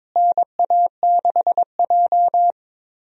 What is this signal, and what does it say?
Morse code, heard as a keyed tone, about 22 wpm, 700 Hz. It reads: NA6J
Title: USA Call Signs - 4 Characters - Repeated 3x - 22wpm